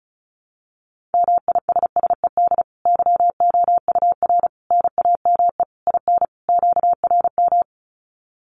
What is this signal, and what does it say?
Morse code, heard as a keyed tone, about 34 wpm, 700 Hz. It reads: MISSED YOUR NAME IN QRM
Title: Common QSO Phrases 35wpm